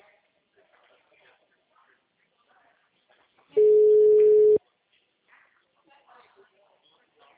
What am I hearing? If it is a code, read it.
Ringback tone (Europe, 425 Hz), with speech.